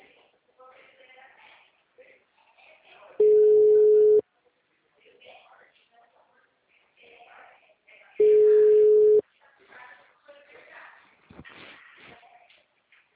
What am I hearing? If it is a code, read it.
Ringback tone (Europe, 425 Hz), with speech.